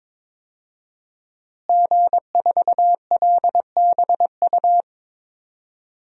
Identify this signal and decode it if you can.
Morse code, heard as a keyed tone, about 22 wpm, 700 Hz. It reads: G4LBU
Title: DXCC-100 Call Signs - 5 Characters - Repeated 3x - 22wpm